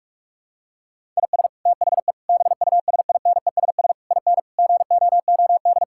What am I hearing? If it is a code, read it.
Morse code, heard as a keyed tone, about 46 wpm, 700 Hz. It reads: IS THE BUSINESS IN GOOD
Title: Sentences from Top 1000 Words - Review Part 5 - 45wpm